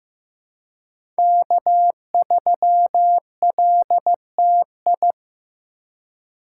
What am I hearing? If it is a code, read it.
Morse code, heard as a keyed tone, about 15 wpm, 700 Hz. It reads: K3LTI
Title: USA Call Signs - 5 Characters - 3x Character Spacing - 15wpm